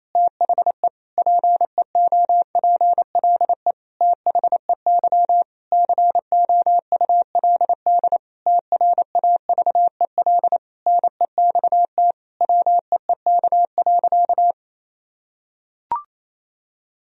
Morse code, heard as a keyed tone, about 27 wpm, 700 Hz. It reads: THE PEOPLE THEY COULD TRAVEL NEXT WEEK.
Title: Sentences from Top 1000 Words - Review Part 2 - 28wpm